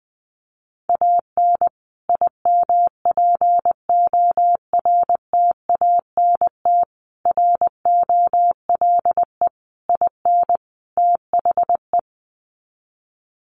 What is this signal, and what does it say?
Morse code, heard as a keyed tone, about 20 wpm, 700 Hz. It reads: AN IMPORTANT ROLE IN THE
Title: Sets of 5 Words Encore2 20wpm